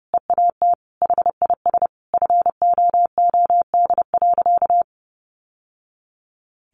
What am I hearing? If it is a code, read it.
Morse code, heard as a keyed tone, about 30 wpm, 700 Hz. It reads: EAT HIS FOOD.
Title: Sentences from Top 1100 Words 30wpm